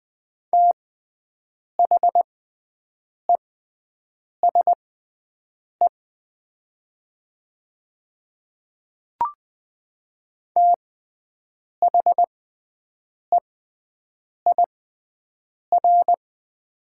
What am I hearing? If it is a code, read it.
Morse code, heard as a keyed tone, about 20 wpm, spaced as for 7 wpm, 700 Hz. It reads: THESE THEIR
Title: 6x Character Spacing - Top 300 Words - 20wpm